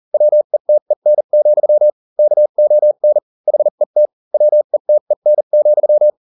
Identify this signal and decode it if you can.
Morse code, heard as a keyed tone, about 40 wpm, spaced as for 35 wpm, 600 Hz. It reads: WETEN, KON HET WETEN,